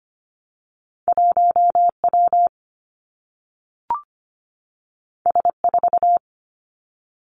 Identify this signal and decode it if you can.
Morse code, heard as a keyed tone, about 25 wpm, 700 Hz. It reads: 1W S4
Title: Two Challenging Letter-Number Combinations 25wpm